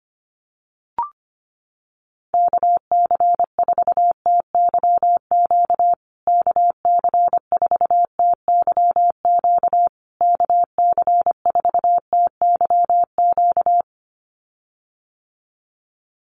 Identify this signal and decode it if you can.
Morse code, heard as a keyed tone, about 25 wpm, 700 Hz. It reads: KC4TYQ KC4TYQ KC4TYQ
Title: USA Call Signs Repeated 3x 25wpm